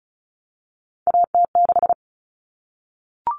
Morse code, heard as a keyed tone, about 34 wpm, 700 Hz. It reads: AT6
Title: Common Cut Numbers - 3-Digit 35wpm